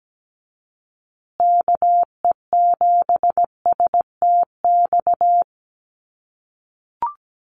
Morse code, heard as a keyed tone, about 17 wpm, 700 Hz. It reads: KE7STX
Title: USA Call Signs Repeated 2x 17wpm